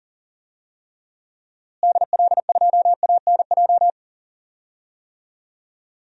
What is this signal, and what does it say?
Morse code, heard as a keyed tone, about 40 wpm, 700 Hz. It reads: DL2ANJ
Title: DXCC-100 Call Signs - 6 Characters - Repeated 3x - 40wpm